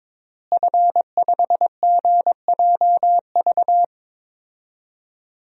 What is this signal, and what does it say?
Morse code, heard as a keyed tone, about 21 wpm, 700 Hz. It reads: F5GJV
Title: DXCC-100 Call Signs - 5 Characters - Repeated 3x - 22wpm